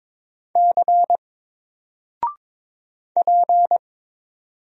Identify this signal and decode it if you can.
Morse code, heard as a keyed tone, about 22 wpm, 700 Hz. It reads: C P
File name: Single-Letter-Number-22wpm